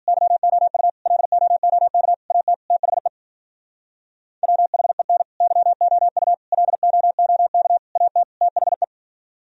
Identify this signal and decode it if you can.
Morse code, heard as a keyed tone, about 49 wpm, 700 Hz. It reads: YOU LOOK AT THE WHEN YOU LOOK AT THE
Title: Speed-Racing - Sets of 5 Words 40wpm